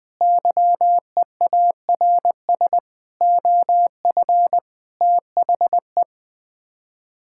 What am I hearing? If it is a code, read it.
Morse code, heard as a keyed tone, about 21 wpm, 700 Hz. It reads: YEARS OF THE
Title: Sets of 6 Words - 20wpm